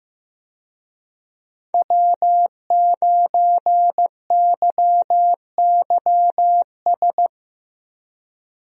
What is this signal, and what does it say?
Morse code, heard as a keyed tone, about 15 wpm, 700 Hz. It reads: W9YYS
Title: USA Call Signs - 5 Characters - Repeated 2x - 15wpm